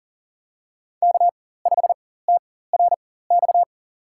Morse code, heard as a keyed tone, about 40 wpm, spaced as for 19 wpm, 700 Hz. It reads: K5TRX
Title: USA Call Signs - 5 Characters - 4x Character Spacing - 40wpm